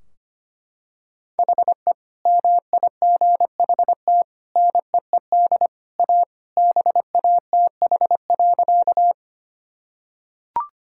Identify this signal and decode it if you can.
Morse code, heard as a keyed tone, about 25 wpm, 700 Hz. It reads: HE MIGHT NEED A BATH.
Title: Sentences from Top 1600 Words 25wpm